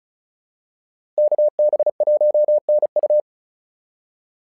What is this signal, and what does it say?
Morse code, heard as a keyed tone, about 36 wpm, 600 Hz. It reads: KB1NU
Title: SOTA QSO Practice 35wpm